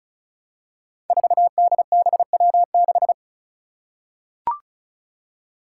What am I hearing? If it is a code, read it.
Morse code, heard as a keyed tone, about 36 wpm, 700 Hz. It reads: 4DBW6
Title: Five Challenging Letter-Number Combinations 35wpm